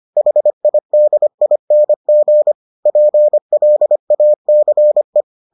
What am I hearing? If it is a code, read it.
Morse code, heard as a keyed tone, about 25 wpm, 600 Hz. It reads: HIDING PLACE